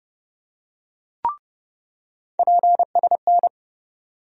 Morse code, heard as a keyed tone, about 30 wpm, 700 Hz. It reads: PSN
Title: USA Call Sign Postfixes - 3 Letters 30wpm